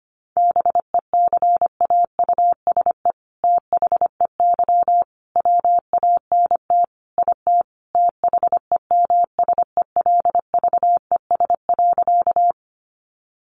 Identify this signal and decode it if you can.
Morse code, heard as a keyed tone, about 25 wpm, 700 Hz. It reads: BECAUSE THEY WANT IT THEMSELVES.
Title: Sentences from Top 800 Words 25wpm